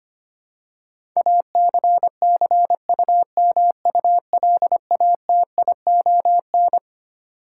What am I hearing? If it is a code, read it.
Morse code, heard as a keyed tone, about 25 wpm, 700 Hz. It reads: ACCUMULATION
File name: Twelve Letter Words - 25wpm